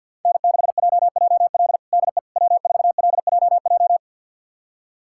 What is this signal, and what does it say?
Morse code, heard as a keyed tone, about 49 wpm, 700 Hz. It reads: N6JJL DE W4LJJ
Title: USA Callsign DE Callsign - 5 Characters - 50wpm